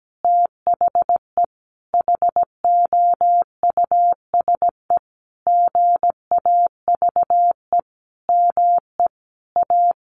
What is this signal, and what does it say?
Morse code, heard as a keyed tone, about 17 wpm, 700 Hz. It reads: THE HOUSE GAVE ME A
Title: Sentences from Top 1500 Words 17wpm